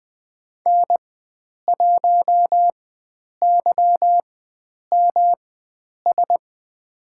Morse code, heard as a keyed tone, about 20 wpm, spaced as for 9 wpm, 700 Hz. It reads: N1YMS
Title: USA Call Signs - 5 Characters - 4x Character Spacing - 20wpm